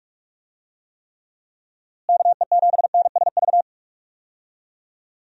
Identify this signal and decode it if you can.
Morse code, heard as a keyed tone, about 41 wpm, 700 Hz. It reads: KE7NSV
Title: USA Call Signs - 6 Characters - Repeated 2x - 45wpm